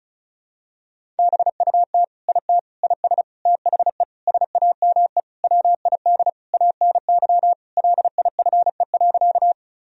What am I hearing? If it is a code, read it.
Morse code, heard as a keyed tone, about 34 wpm, 700 Hz. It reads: BUT IT IS THE SAME WID ANY LIFE.
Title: Sentences from Top 300 Words and Abbreviations 35wpm